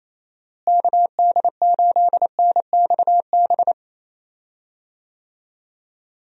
Morse code, heard as a keyed tone, about 27 wpm, 700 Hz. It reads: KD8NXB
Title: USA Call Signs - 6 Characters - Repeated 3x - 28wpm